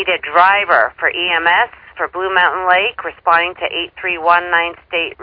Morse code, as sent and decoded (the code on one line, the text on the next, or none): none